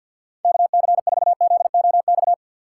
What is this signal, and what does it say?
Morse code, heard as a keyed tone, about 52 wpm, 700 Hz. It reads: KC4ZOX
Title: USA Call Signs - 6 Characters - 6x Character Spacing - 50wpm